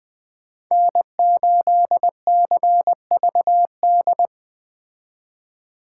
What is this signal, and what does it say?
Morse code, heard as a keyed tone, about 20 wpm, 700 Hz. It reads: N8CVD